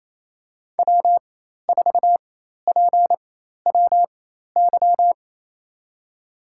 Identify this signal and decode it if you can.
Morse code, heard as a keyed tone, about 28 wpm, spaced as for 13 wpm, 700 Hz. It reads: W4PWY
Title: USA Call Signs - 5 Characters - 4x Character Spacing - 28wpm